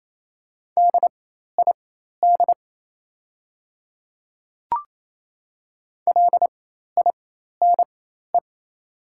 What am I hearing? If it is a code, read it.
Morse code, heard as a keyed tone, about 28 wpm, spaced as for 13 wpm, 700 Hz. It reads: DID LINE
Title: 4x Character Spacing - Top 300 Words - 28wpm